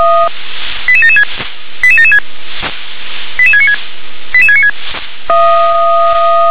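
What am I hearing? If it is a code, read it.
Five-tone selective call (ZVEI), heard as quick runs of five tones, tones of about 70 ms. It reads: 88776 88776 88696 88696
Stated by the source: ZVEI five-tone selective call received on SDR.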